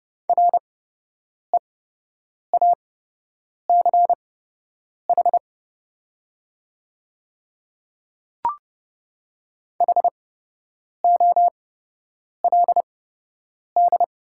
Morse code, heard as a keyed tone, about 30 wpm, spaced as for 8 wpm, 700 Hz. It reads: REACH HOLD
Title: 8x Character Spacing - Top 500 Words - 30wpm